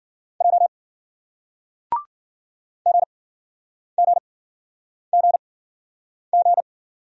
Morse code, heard as a keyed tone, about 51 wpm, spaced as for 9 wpm, 700 Hz. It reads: J GGGG